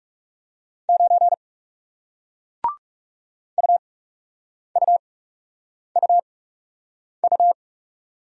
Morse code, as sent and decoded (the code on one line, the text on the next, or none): ----. / ..- ..- ..- ..-
9 UUUU